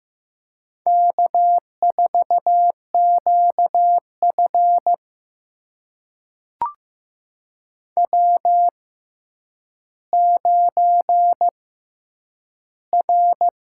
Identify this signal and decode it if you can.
Morse code, heard as a keyed tone, about 15 wpm, spaced as for 9 wpm, 700 Hz. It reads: K4QF W9R
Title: USA Call Signs - 4 Characters - 6x Character Spacing - 15wpm